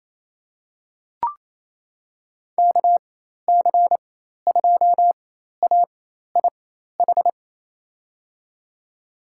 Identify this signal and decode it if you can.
Morse code, heard as a keyed tone, about 28 wpm, spaced as for 13 wpm, 700 Hz. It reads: KC2AIH
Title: USA Call Signs - 6 Characters - 4x Character Spacing - 28wpm